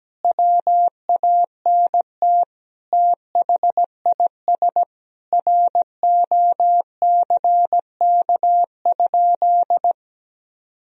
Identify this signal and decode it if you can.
Morse code, heard as a keyed tone, about 17 wpm, 700 Hz. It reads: WANT THIS ROCK?
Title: Sentences from Top 1000 Words - Review Part 2 - 17wpm